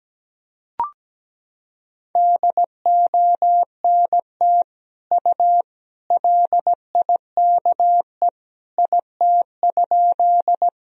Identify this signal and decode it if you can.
Morse code, heard as a keyed tone, about 17 wpm, 700 Hz. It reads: DONT U LIKE IT?
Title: Sentences from Top 300 Words and Abbreviations 17wpm